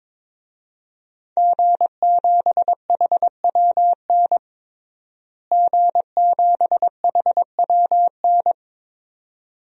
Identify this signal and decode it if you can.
Morse code, heard as a keyed tone, about 22 wpm, 700 Hz. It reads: G7HWN G7HWN
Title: DXCC-100 Call Signs - 5 Characters - Repeated 2x - 22wpm